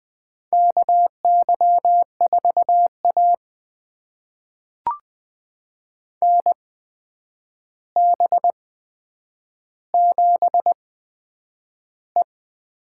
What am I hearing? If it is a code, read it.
Morse code, heard as a keyed tone, about 20 wpm, spaced as for 9 wpm, 700 Hz. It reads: KY4A NB7E
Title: USA Call Signs - 4 Characters - 8x Character Spacing - 20wpm